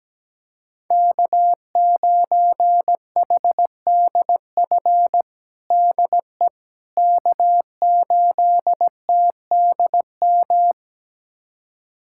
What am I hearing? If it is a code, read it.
Morse code, heard as a keyed tone, about 17 wpm, 700 Hz. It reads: K9HDF DE K8TDM
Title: USA Callsign DE Callsign - 5 Characters - 17wpm